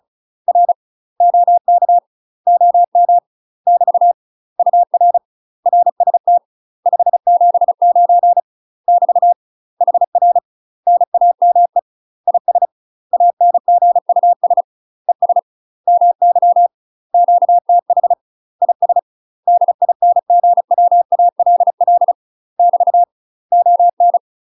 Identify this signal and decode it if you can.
Morse code, heard as a keyed tone, about 34 wpm, 700 Hz. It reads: R OK OM <BT> UR RST 579 <BT> HR NAME IS ANGUS ES MY QTH IS DINGWALL <BT> ON
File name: UK MASTER QSOs TRAINING / 35WPM / UK MASTER QSOs 0028